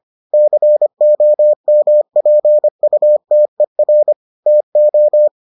Morse code, heard as a keyed tone, about 25 wpm, 600 Hz. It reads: COMPUTER TO